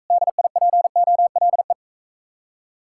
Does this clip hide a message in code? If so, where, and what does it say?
Yes: Morse code from 0.1 to 1.7 s, heard as a keyed tone, about 45 wpm, 700 Hz. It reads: DIPOLE